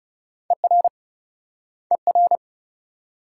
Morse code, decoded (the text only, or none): ER ER